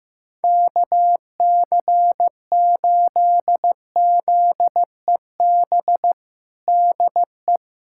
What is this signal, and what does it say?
Morse code, heard as a keyed tone, about 16 wpm, 700 Hz. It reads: KC8ZEB DE
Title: USA Callsign DE Callsign 15wpm